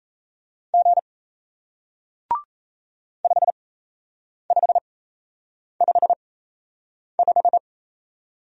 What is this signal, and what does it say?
Morse code, heard as a keyed tone, about 38 wpm, spaced as for 8 wpm, 700 Hz. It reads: G 5555